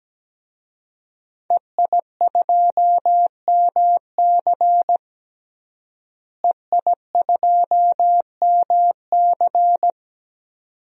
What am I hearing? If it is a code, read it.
Morse code, heard as a keyed tone, about 17 wpm, 700 Hz. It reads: EI2MC EI2MC
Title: DXCC-100 Call Signs - 5 Characters - Repeated 2x - 17wpm